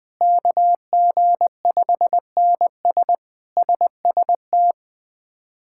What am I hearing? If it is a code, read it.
Morse code, heard as a keyed tone, about 21 wpm, 700 Hz. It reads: KG5NS SST
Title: K1USN SST QSO Practice - 20wpm